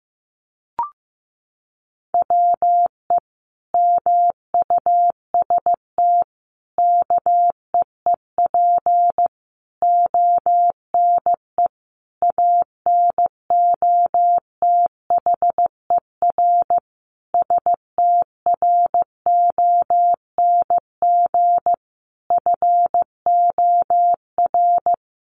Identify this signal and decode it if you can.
Morse code, heard as a keyed tone, about 15 wpm, 700 Hz. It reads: WE MUST KEEP ONE ANOTHER STRONG FOR
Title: Sentences from Top 600 Words 15wpm